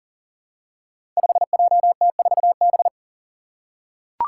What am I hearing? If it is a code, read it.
Morse code, heard as a keyed tone, about 40 wpm, 700 Hz. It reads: 5JT4B
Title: Five Challenging Letter-Number Combinations 40wpm